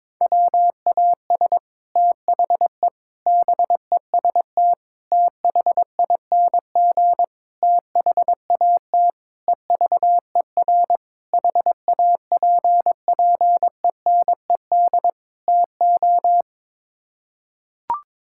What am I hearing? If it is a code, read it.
Morse code, heard as a keyed tone, about 22 wpm, 700 Hz. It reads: WAS THE BEST THING THAT EVER HAPPENED TO
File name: practice / 22wpm / Sets of 8 Words - 22wpm